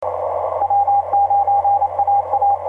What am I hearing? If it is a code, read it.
Morse code, heard as a keyed tone, about 27 wpm, 800 Hz, with radio noise. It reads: W9AA
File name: pj2-wi9wi w9aa etc 30m cw 24nov2010 2138